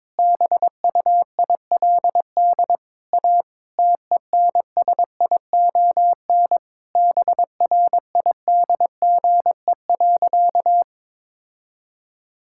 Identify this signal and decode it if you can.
Morse code, heard as a keyed tone, about 21 wpm, 700 Hz. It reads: BUILD A TENSION BRIDGE.
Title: Sentences from Top 1600 Words 22wpm